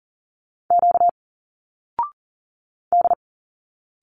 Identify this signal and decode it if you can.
Morse code, heard as a keyed tone, about 40 wpm, 700 Hz. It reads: Q D
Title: Single Letters 40wpm